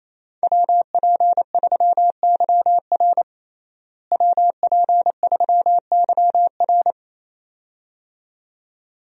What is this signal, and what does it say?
Morse code, heard as a keyed tone, about 28 wpm, 700 Hz. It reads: WP3YR WP3YR
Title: USA Call Signs - 5 Characters - Repeated 2x - 28wpm